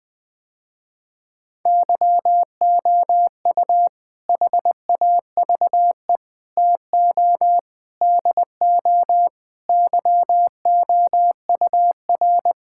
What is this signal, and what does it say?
Morse code, heard as a keyed tone, about 20 wpm, 700 Hz. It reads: YOU HAVE TO DO YOUR
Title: Sentences with 4-Word Phrases - Spoken Beginning - 20wpm